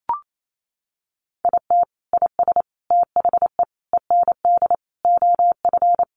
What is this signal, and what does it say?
Morse code, heard as a keyed tone, about 28 wpm, 700 Hz. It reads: IT IS THE END OF
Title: Sentences from Top 1500 Words 28wpm